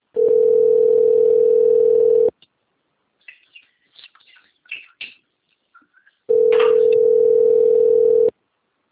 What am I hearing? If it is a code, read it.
Ringback tone (North America), 440 + 480 Hz.